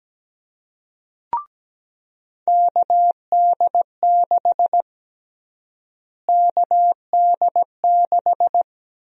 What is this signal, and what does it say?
Morse code, heard as a keyed tone, about 17 wpm, 700 Hz. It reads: KD6 KD6
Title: USA Call Sign Prefixes - Repeated 2x - 17wpm